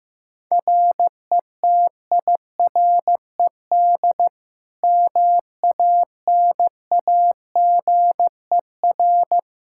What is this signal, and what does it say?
Morse code, heard as a keyed tone, about 15 wpm, 700 Hz. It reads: RETIRED MANAGER
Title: K1USN SST QSO Practice 15wpm